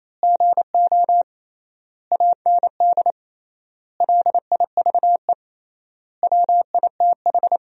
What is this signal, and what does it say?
Morse code, heard as a keyed tone, about 28 wpm, 700 Hz. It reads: GO AND LIVE WITH
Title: 3x Word-Spacing - Sentences from Top 500 Words - 28wpm